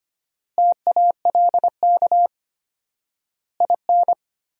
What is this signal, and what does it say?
Morse code, heard as a keyed tone, about 25 wpm, 700 Hz. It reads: TALK IN
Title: 4x Word-Spacing - Sentences from Top 500 Words - 25wpm